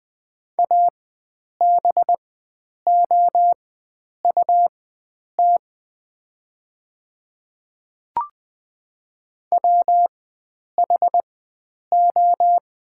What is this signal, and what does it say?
Morse code, heard as a keyed tone, about 20 wpm, spaced as for 9 wpm, 700 Hz. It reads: ABOUT WHO
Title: 4x Character Spacing - Top 100 Words - 20wpm